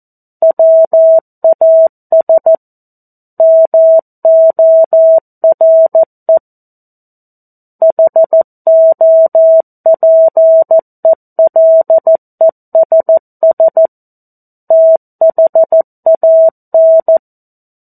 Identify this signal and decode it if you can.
Morse code, heard as a keyed tone, about 14 wpm, 650 Hz. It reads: WAS MORE HOPELESS THAN
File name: 02-Chapter-02 14 WPM